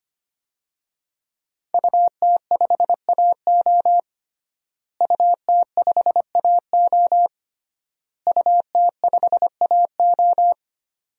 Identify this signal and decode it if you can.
Morse code, heard as a keyed tone, about 25 wpm, 700 Hz. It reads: UT5AO UT5AO UT5AO
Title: DXCC-100 Call Signs - 5 Characters - Repeated 3x - 25wpm